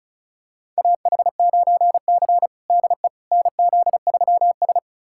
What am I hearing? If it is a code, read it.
Morse code, heard as a keyed tone, about 36 wpm, 700 Hz. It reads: AH9C DE NZ3S
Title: USA Callsign DE Callsign - 4 Characters - 35wpm